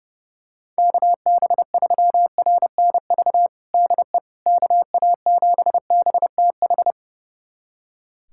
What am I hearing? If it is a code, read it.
Morse code, heard as a keyed tone, about 30 wpm, 690 Hz. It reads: KB3RNV DE KA7BTH
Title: USA Callsign DE Callsign 30wpm